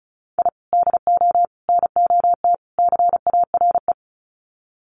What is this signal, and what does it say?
Morse code, heard as a keyed tone, about 34 wpm, 700 Hz. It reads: I DO NOT CARE
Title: Sets of 4 Words 35wpm